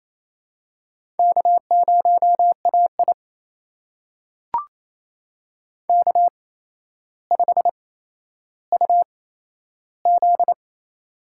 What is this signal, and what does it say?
Morse code, heard as a keyed tone, about 28 wpm, spaced as for 12 wpm, 700 Hz. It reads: K0AI K5UZ
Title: USA Call Signs - 4 Characters - 8x Character Spacing - 28wpm